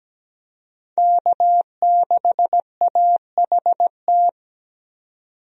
Morse code, heard as a keyed tone, about 17 wpm, 700 Hz. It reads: K6AHT
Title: USA Call Signs - 5 Characters - 17wpm